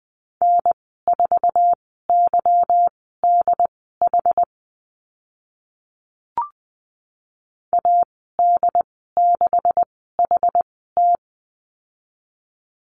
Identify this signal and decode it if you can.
Morse code, heard as a keyed tone, about 20 wpm, spaced as for 14 wpm, 700 Hz. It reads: N4YDH AD6HT
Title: 161 Lesson 26g - Letter g - Call Signs 20wpm